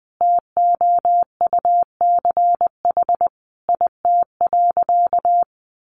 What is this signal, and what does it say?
Morse code, heard as a keyed tone, about 20 wpm, 700 Hz. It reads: TOUCH IT.
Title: Sentences from Top 1200 Words 20wpm